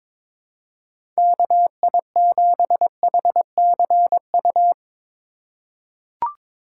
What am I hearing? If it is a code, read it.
Morse code, heard as a keyed tone, about 22 wpm, 700 Hz. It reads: KI7HCU
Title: USA Call Signs - 6 Characters - 2x Character Spacing - 22wpm